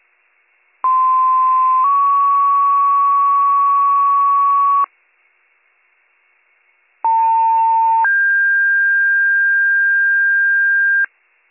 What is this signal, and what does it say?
Two-tone page, heard as a one-second tone, then a three-second tone, about 1040 then 1120 Hz, about 900 then 1600 Hz. It is heard from 0.8 to 11.1 s.